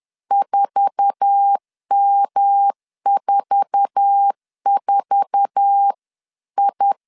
Morse code, heard as a keyed tone, about 10 wpm, 800 Hz. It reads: 4M44I
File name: LEZIONI / 8 YJ45 / 8 YJ4 10wpm 2633